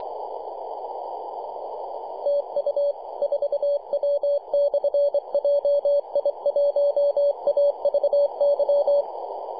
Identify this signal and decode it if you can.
Morse code, heard as a keyed tone, about 25 wpm, 570 Hz. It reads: TU 4W/JI1AVY
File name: 2229z 4W/JI1AVY 21.023 CW